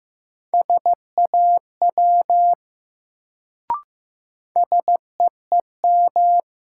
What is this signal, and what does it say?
Morse code, heard as a keyed tone, about 16 wpm, 700 Hz. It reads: SAW SEEM
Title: Sentences from Top 300 Words - New Word Drill - 15wpm